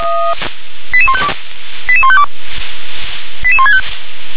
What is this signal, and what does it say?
Five-tone selective call (ZVEI), heard as quick runs of five tones, tones of about 70 ms. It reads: 88152 88152 88165
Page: SDR/ZVEI